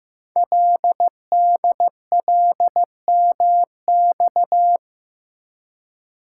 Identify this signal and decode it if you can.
Morse code, heard as a keyed tone, about 15 wpm, 700 Hz. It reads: LDLMX
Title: Code Groups - 5 Letters - 15wpm